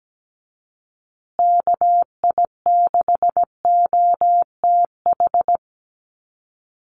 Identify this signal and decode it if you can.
Morse code, heard as a keyed tone, about 17 wpm, 700 Hz. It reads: KI6OTH
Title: USA Call Signs Repeated 2x 17wpm